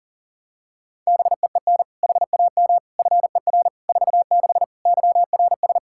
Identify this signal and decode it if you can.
Morse code, heard as a keyed tone, about 40 wpm, 700 Hz. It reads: BEEN HAM FER 46 YRS
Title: K1USN SST QSO Practice 40wpm